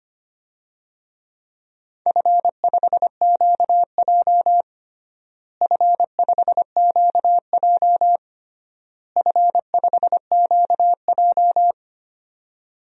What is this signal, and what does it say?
Morse code, heard as a keyed tone, about 25 wpm, 700 Hz. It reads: F5QJ F5QJ F5QJ
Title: DXCC-100 Call Signs - 4 Characters - Repeated 3x - 25wpm